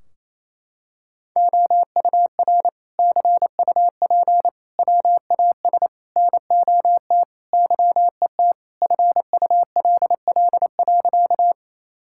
Morse code, heard as a keyed tone, about 27 wpm, 700 Hz. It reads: OUR CUP WAS NOT YET FULL.
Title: Sentences from Top 1600 Words 28wpm